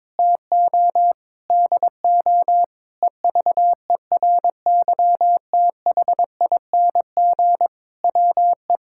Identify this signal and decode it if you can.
Morse code, heard as a keyed tone, about 22 wpm, 700 Hz. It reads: TO DO EVERYTHING WE
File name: practice / 22wpm / Sentences with 8-Word Phrases - 22wpm